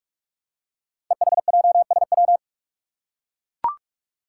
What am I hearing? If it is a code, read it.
Morse code, heard as a keyed tone, about 44 wpm, 700 Hz. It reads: EHJSW
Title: Code Groups - 5 Letters - 45wpm